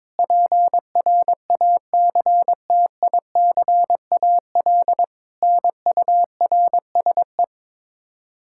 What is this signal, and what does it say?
Morse code, heard as a keyed tone, about 22 wpm, 700 Hz. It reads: PRACTICAL NURSE
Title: Practice QSO - Exchange 3 - More Info - 22wpm